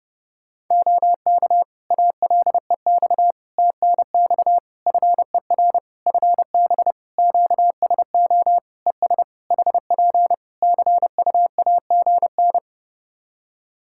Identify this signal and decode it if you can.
Morse code, heard as a keyed tone, about 30 wpm, 700 Hz. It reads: OK ALEX TNX FER FB QSO ES HP CUAGN
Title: Practice QSO - Exchange 4 - Ending - 30wpm